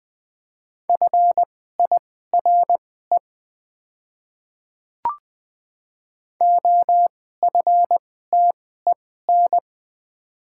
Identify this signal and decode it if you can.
Morse code, heard as a keyed tone, about 20 wpm, spaced as for 14 wpm, 700 Hz. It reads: FIRE OFTEN
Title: 2x Character Spacing - Top 500 Words - 20wpm